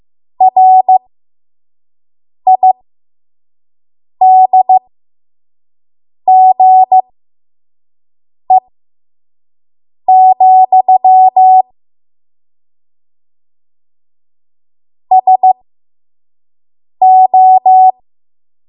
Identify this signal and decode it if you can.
Morse code, heard as a keyed tone, about 15 wpm, spaced as for 5 wpm, 750 Hz. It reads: RIDGE, SO
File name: Morse / 05WPM